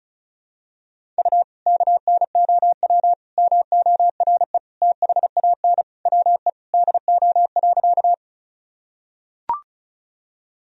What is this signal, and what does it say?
Morse code, heard as a keyed tone, about 34 wpm, 700 Hz. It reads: U KNOW MORE THAN WE DO.
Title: Sentences from Top 100 Words and Abbreviations 35wpm